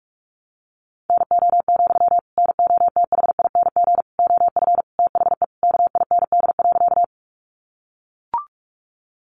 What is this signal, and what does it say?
Morse code, heard as a keyed tone, about 44 wpm, 700 Hz. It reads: NO, NOTHING OF THE KIND.
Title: Sentences from Top 500 Words 45wpm